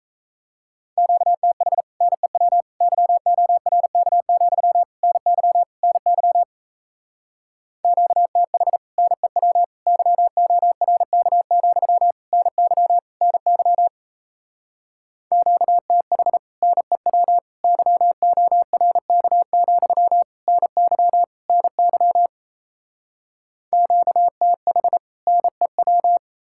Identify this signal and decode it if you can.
Morse code, heard as a keyed tone, about 37 wpm, 700 Hz. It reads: QTH NEW YORK, NY NY QTH NEW YORK, NY NY QTH NEW YORK, NY NY QTH NEW